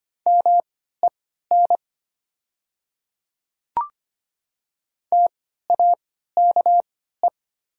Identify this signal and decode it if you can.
Morse code, heard as a keyed tone, about 25 wpm, spaced as for 14 wpm, 700 Hz. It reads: MEN TAKE
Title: 3x Character Spacing - Top 200 Words - 25wpm